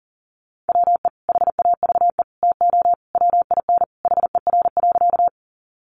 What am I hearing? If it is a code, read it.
Morse code, heard as a keyed tone, about 40 wpm, 700 Hz. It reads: WE HAVE TO WIN HER.
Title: Sentences from Top 1500 Words 40wpm